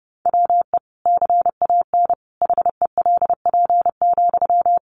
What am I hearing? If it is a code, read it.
Morse code, heard as a keyed tone, about 30 wpm, 700 Hz. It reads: WE CAN HELP,
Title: Sentences from Top 1500 Words 30wpm